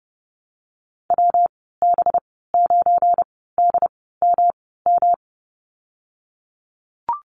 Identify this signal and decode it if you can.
Morse code, heard as a keyed tone, about 30 wpm, spaced as for 17 wpm, 700 Hz. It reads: WB9DMM
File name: Lesson 37g - Letter x - Call Signs 30wpm